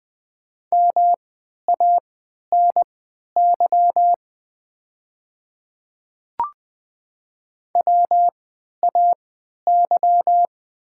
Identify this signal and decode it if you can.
Morse code, heard as a keyed tone, about 20 wpm, spaced as for 11 wpm, 700 Hz. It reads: MANY WAY
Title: 3x Character Spacing - Top 100 Words - 20wpm